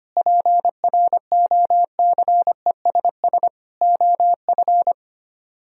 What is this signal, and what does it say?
Morse code, heard as a keyed tone, about 31 wpm, 700 Hz. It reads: PROCESS OF
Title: Speed-Racing - Sets of 4 Words 25wpm